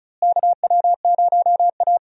Morse code, heard as a keyed tone, about 36 wpm, 700 Hz. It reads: KW0A
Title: USA Call Signs - 4 Characters - Repeated 3x - 35wpm